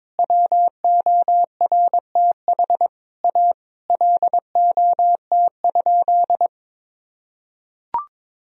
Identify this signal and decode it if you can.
Morse code, heard as a keyed tone, about 22 wpm, 700 Hz. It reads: WORTH A LOT?
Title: Sentences from Top 1000 Words - Review Part 4 - 22wpm